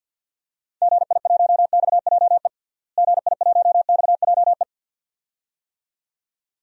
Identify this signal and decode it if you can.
Morse code, heard as a keyed tone, about 50 wpm, 700 Hz. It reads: GI1XJE GI1XJE
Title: DXCC-100 Call Signs - 6 Characters - Repeated 2x - 50wpm